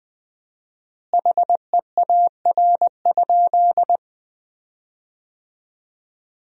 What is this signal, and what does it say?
Morse code, heard as a keyed tone, about 21 wpm, 700 Hz. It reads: HEAR?
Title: 4x Word-Spacing - Sentences from Top 300 Words - 20wpm